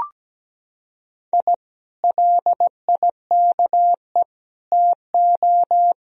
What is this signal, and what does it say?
Morse code, heard as a keyed tone, about 17 wpm, 700 Hz. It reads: I LIKE TO
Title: Sentences from Top 1000 Words - Review Part 3 - 17wpm